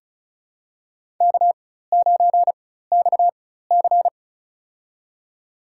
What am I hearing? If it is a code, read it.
Morse code, heard as a keyed tone, about 34 wpm, spaced as for 16 wpm, 700 Hz. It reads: K9XC
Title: USA Call Signs - 4x Character Spacing - 35wpm